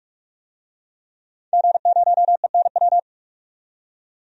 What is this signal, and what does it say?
Morse code, heard as a keyed tone, about 46 wpm, 700 Hz. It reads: G0ENW
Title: DXCC-100 Call Signs - 5 Characters - 45wpm